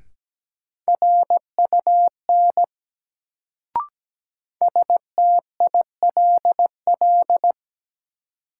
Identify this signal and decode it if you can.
Morse code, heard as a keyed tone, about 17 wpm, 700 Hz. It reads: RUN STILL
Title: Sentences from Top 300 Words - New Word Drill - 17wpm